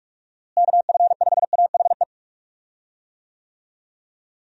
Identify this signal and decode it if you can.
Morse code, heard as a keyed tone, about 46 wpm, 700 Hz. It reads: KF5AHE
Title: USA Call Signs - 6 Characters - Repeated 2x - 45wpm